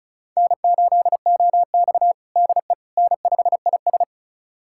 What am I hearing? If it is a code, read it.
Morse code, heard as a keyed tone, about 36 wpm, 700 Hz. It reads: N8OX DE N5IS
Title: USA Callsign DE Callsign - 4 Characters - 35wpm